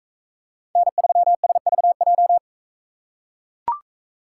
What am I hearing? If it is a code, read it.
Morse code, heard as a keyed tone, about 43 wpm, 700 Hz. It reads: N3SVJ